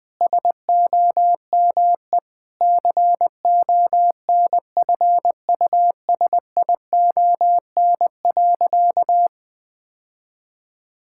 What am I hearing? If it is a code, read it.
Morse code, heard as a keyed tone, about 20 wpm, 700 Hz. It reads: SOME CONFUSION.
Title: Sentences from Top 1600 Words 20wpm